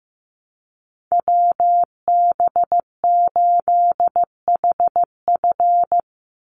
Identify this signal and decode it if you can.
Morse code, heard as a keyed tone, about 15 wpm, 700 Hz. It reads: WB8HF
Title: Call Signs Repeated 3x - 15wpm